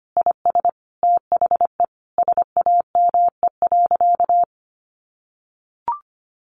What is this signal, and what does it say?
Morse code, heard as a keyed tone, about 25 wpm, 700 Hz. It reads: IS THE SAME.
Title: Sentences from Top 1400 Words 25wpm